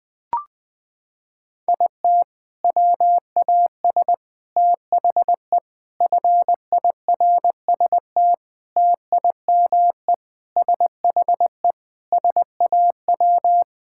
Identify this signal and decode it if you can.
Morse code, heard as a keyed tone, about 20 wpm, 700 Hz. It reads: IT WAS THE FIRST TIME SHE SAW
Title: Sentences with 5-Word Phrases - 20wpm